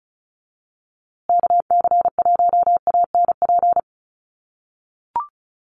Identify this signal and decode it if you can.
Morse code, heard as a keyed tone, about 34 wpm, 700 Hz. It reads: KC1ANP
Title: USA Call Signs Repeated 2x 35wpm